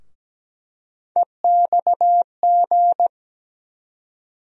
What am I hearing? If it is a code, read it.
Morse code, heard as a keyed tone, about 17 wpm, 700 Hz. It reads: EXG
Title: USA Call Sign Postfixes - 3 Letters 17wpm